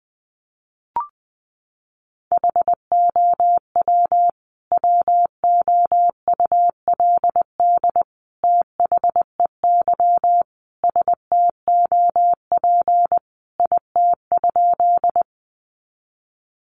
Morse code, heard as a keyed tone, about 20 wpm, 700 Hz. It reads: HOW WOULD THEY STOP IT?
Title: Sentences from Top 300 Words 20wpm